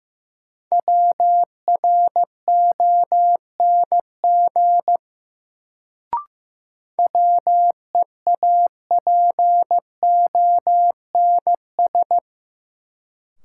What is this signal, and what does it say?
Morse code, heard as a keyed tone, about 15 wpm, 700 Hz. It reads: WRONG WEAPONS
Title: Sets of 5 Words Encore - New Word Drill - 15wpm